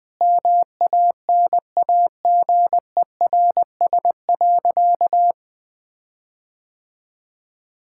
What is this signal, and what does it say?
Morse code, heard as a keyed tone, about 21 wpm, 700 Hz. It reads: MANAGERS.
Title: Sentences with 6-Word Phrases - 20wpm